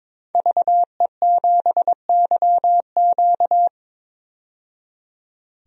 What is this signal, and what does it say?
Morse code, heard as a keyed tone, about 22 wpm, 700 Hz. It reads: VE7YQ
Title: DXCC-100 Call Signs - 5 Characters - 22wpm